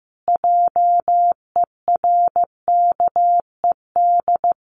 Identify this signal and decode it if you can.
Morse code, heard as a keyed tone, about 15 wpm, 700 Hz. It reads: JERKED
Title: A Sound Of Thunder 15wpm